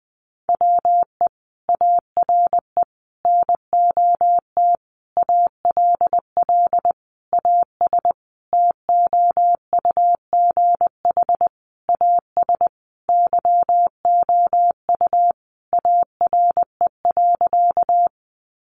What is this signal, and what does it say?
Morse code, heard as a keyed tone, about 20 wpm, 700 Hz. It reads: WE ARE NOT ALL AS TOUGH AS YOU ARE.
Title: Sentences from Top 1200 Words 20wpm